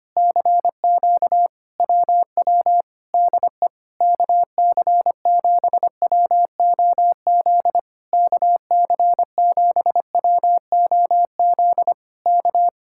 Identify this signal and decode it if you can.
Morse code, heard as a keyed tone, about 24 wpm, 700 Hz. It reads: CQ WW DE KC7WOZ KC7WOZ K
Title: Speed-Racing Common QSO Phrases - 25wpm